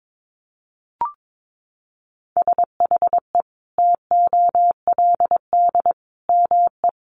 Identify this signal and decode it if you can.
Morse code, heard as a keyed tone, about 22 wpm, 700 Hz. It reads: SHE TOLD ME
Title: Sentences from Top 400 Words 22wpm